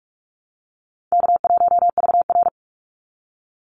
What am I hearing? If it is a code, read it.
Morse code, heard as a keyed tone, about 44 wpm, 700 Hz. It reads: K1VR
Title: Easy Call Signs - 45wpm